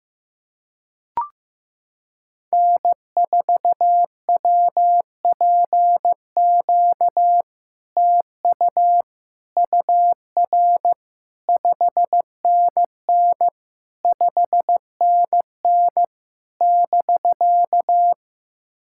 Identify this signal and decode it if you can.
Morse code, heard as a keyed tone, about 15 wpm, 700 Hz. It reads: N4WPQ TU UR 5NN 5NN <BK>
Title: SOTA QSO Practice 15wpm